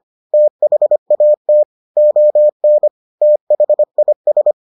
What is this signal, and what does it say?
Morse code, heard as a keyed tone, about 25 wpm, 600 Hz. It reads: THAT ON THIS